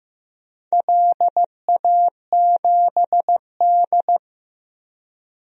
Morse code, heard as a keyed tone, about 15 wpm, 700 Hz. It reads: LA7D
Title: DXCC-100 Call Signs - 4 Characters - Repeated 2x - 15wpm